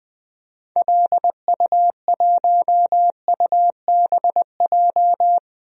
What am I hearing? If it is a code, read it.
Morse code, heard as a keyed tone, about 21 wpm, 700 Hz. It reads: LU1UBJ
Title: DXCC-100 Call Signs - 6 Characters - 20wpm